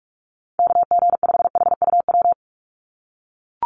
Morse code, heard as a keyed tone, about 44 wpm, 700 Hz. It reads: KG5HUW
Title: USA Call Signs 45wpm